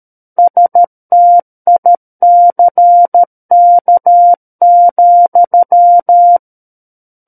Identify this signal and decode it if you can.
Morse code, heard as a keyed tone, about 13 wpm, 700 Hz. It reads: STICK,
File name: Book One / 04-Chapter 4 - 13 WPM